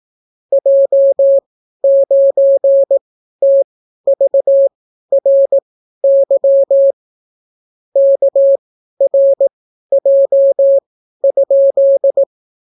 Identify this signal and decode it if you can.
Morse code, heard as a keyed tone, about 18 wpm, spaced as for 12 wpm, 550 Hz. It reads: J9TVRY KRJ?